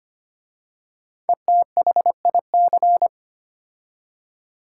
Morse code, heard as a keyed tone, about 25 wpm, 700 Hz. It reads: ETHIC
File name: 5 Letter Words - First 3 Letters Given - 25wpm